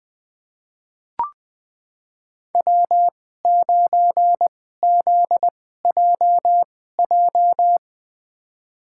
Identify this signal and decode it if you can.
Morse code, heard as a keyed tone, about 20 wpm, spaced as for 14 wpm, 700 Hz. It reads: W9ZJJ
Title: USA Call Signs - 5 Characters - 2x Character Spacing - 20wpm